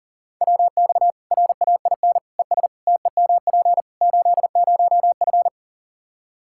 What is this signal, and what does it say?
Morse code, heard as a keyed tone, about 40 wpm, 700 Hz. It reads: WX RAIN ES TEMP 80F
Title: K1USN SST QSO Practice 40wpm